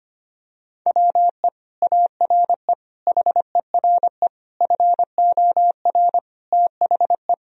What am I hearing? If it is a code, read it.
Morse code, heard as a keyed tone, about 25 wpm, 700 Hz. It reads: WE ARE HERE FOR THE
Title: Sentences from Top 1000 Words - Review Part 4 - 25wpm